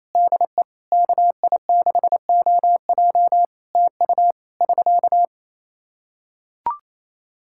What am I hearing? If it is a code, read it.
Morse code, heard as a keyed tone, about 28 wpm, 700 Hz. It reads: DE KI6OJ TU <SK>
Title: Practice QSO - Exchange 4 - Ending - 28wpm